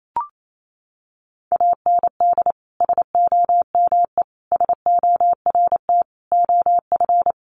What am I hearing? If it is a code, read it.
Morse code, heard as a keyed tone, about 28 wpm, 700 Hz. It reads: AND SOME SORT OF
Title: Sentences from Top 1300 Words 28wpm